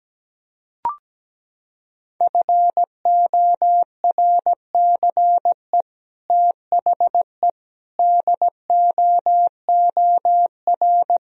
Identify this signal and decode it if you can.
Morse code, heard as a keyed tone, about 17 wpm, 700 Hz. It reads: FORCE THE DOOR
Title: Sentences from Top 1000 Words - Review Part 3 - 17wpm